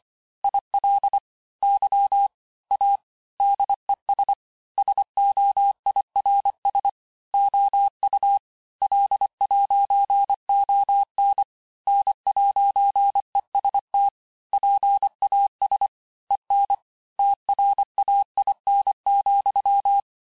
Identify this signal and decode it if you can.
Morse code, heard as a keyed tone, about 25 wpm, 800 Hz. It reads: IL Y A DES SOIRS OU L'ON N'EST PAS EN TRAIN,
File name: cours-868-teste-clair-vitesse-23mpm